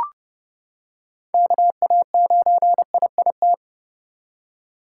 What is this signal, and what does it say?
Morse code, heard as a keyed tone, about 31 wpm, 700 Hz. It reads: KA9IIT